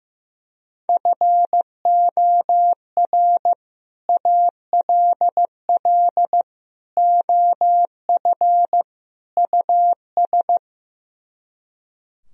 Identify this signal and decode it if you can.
Morse code, heard as a keyed tone, about 15 wpm, 700 Hz. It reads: FOR ALL OF US
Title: Speed-Racing - Sets of 4 Words 15wpm